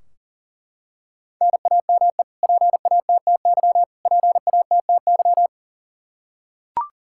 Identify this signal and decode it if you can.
Morse code, heard as a keyed tone, about 40 wpm, 700 Hz. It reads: NAME PATTY PATTY
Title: Practice QSO - Exchange 1 - Beginning - 40wpm